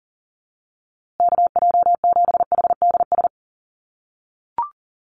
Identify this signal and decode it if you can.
Morse code, heard as a keyed tone, about 40 wpm, 700 Hz. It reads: KJ7HDS
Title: USA Call Signs Repeated 3x 40wpm